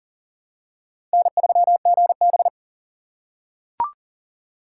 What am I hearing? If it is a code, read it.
Morse code, heard as a keyed tone, about 40 wpm, 690 Hz. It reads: N3GB
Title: USA Call Signs - 4 Characters - 2x Character Spacing - 40wpm